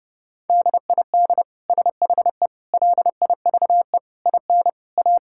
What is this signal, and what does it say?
Morse code, heard as a keyed tone, about 30 wpm, 700 Hz. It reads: DID SHE LIVE IN A
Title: Sentences from Top 1000 Words - Review Part 5 - 30wpm